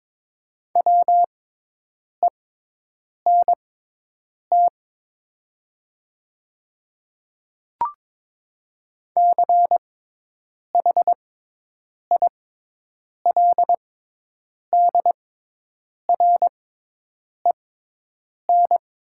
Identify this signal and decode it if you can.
Morse code, heard as a keyed tone, about 22 wpm, spaced as for 8 wpm, 700 Hz. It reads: WENT CHILDREN
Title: 6x Character Spacing - Top 500 Words - 22wpm